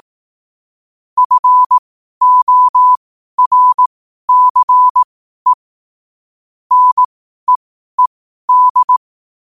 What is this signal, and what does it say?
Morse code, heard as a keyed tone, about 18 wpm, spaced as for 12 wpm, 1000 Hz. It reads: FORCE NEED
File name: qotdfiles / 12wpm